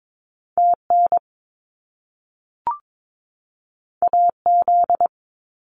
Morse code, heard as a keyed tone, about 22 wpm, 700 Hz. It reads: TN AZ
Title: US States 22wpm